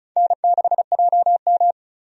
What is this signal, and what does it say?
Morse code, heard as a keyed tone, about 36 wpm, 700 Hz. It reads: N6JM
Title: USA Call Signs - 4 Characters - 35wpm